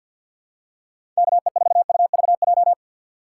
Morse code, heard as a keyed tone, about 53 wpm, 700 Hz. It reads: KE4UVJ
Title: USA Call Signs - 6 Characters - 3x Character Spacing - 50wpm